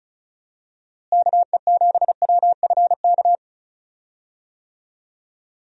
Morse code, heard as a keyed tone, about 36 wpm, 700 Hz. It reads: KE7WFK
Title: USA Call Signs - 6 Characters - Repeated 2x - 35wpm